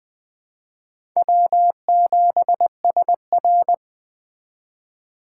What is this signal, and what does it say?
Morse code, heard as a keyed tone, about 20 wpm, 700 Hz. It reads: W7SR
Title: Speed Racing - USA 1x2 Call Signs 17wpm